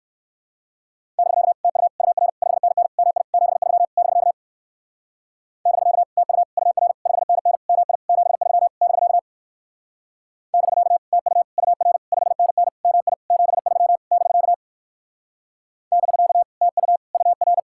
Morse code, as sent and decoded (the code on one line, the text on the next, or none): -...-.- / - ..- / ..- .-. / ..... -. -. / -- .. / --... ...-- / -...-.- / -...-.- / - ..- / ..- .-. / ..... -. -. / -- .. / --... ...-- / -...-.- / -...-.- / - ..- / ..- .-. / ..... -. -. / -- .. / --... ...-- / -...-.- / -...-.- / - ..- / ..- .-.
<BK> TU UR 5NN MI 73 <BK> <BK> TU UR 5NN MI 73 <BK> <BK> TU UR 5NN MI 73 <BK> <BK> TU UR